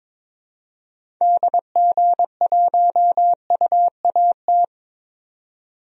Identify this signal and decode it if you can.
Morse code, heard as a keyed tone, about 21 wpm, 700 Hz. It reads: DG1UAT
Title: DXCC-100 Call Signs - 6 Characters - 22wpm